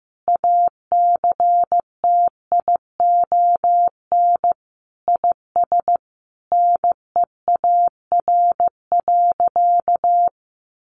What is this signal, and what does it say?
Morse code, heard as a keyed tone, about 15 wpm, 700 Hz. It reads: ACTION IS NEAR.
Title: Sentences from Top 700 Words 15wpm